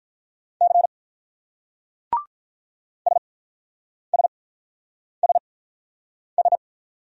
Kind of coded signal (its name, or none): Morse code